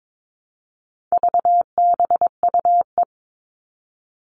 Morse code, heard as a keyed tone, about 22 wpm, 700 Hz. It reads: VBUE
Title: All Cut Numbers - 4-Digit 22wpm